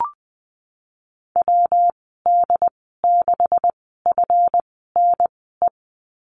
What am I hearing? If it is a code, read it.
Morse code, heard as a keyed tone, about 20 wpm, spaced as for 14 wpm, 700 Hz. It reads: WD6FNE
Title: 147 Lesson 24g - Letter y - Call Signs 20wpm